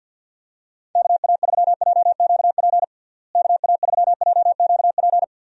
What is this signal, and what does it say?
Morse code, heard as a keyed tone, about 50 wpm, 700 Hz. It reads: KA3JQP KA3JQP
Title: USA Call Signs - 6 Characters - Repeated 3x - 50wpm